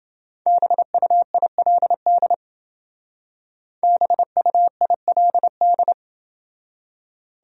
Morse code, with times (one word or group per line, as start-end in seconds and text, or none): 0.5-2.3 BUILD
3.8-5.9 BUILD